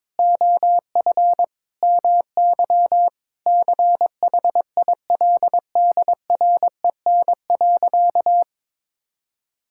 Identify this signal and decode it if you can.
Morse code, heard as a keyed tone, about 22 wpm, 700 Hz. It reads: OF MY CHILDREN.
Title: Sentences from Top 1000 Words - Review Part 5 - 22wpm